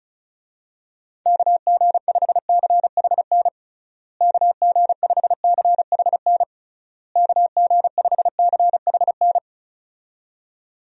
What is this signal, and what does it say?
Morse code, heard as a keyed tone, about 34 wpm, 700 Hz. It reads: KG5CHN KG5CHN KG5CHN
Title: USA Call Signs - 6 Characters - Repeated 3x - 35wpm